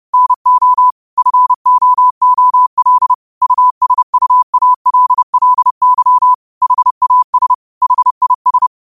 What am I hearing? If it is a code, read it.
Morse code, heard as a keyed tone, about 30 wpm, 1000 Hz. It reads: NO FOOL USUALLY HAS HIS
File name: qotdfiles / 30wpm